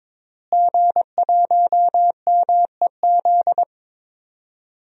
Morse code, heard as a keyed tone, about 22 wpm, 700 Hz. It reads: G1MEZ
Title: DXCC-100 Call Signs - 5 Characters - Repeated 2x - 22wpm